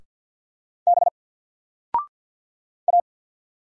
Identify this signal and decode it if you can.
Morse code, heard as a keyed tone, about 50 wpm, 700 Hz. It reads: B A
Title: Single Letter-Number 50wpm